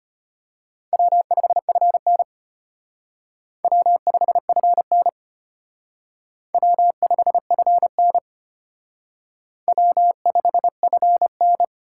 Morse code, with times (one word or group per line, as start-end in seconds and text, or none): 0.9-2.2 W5FN
3.6-5.1 W5FN
6.5-8.2 W5FN
9.7-11.6 W5FN